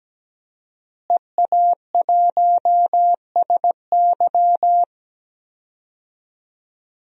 Morse code, heard as a keyed tone, about 17 wpm, 700 Hz. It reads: EA1SY
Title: DXCC-100 Call Signs - 5 Characters - Repeated 3x - 17wpm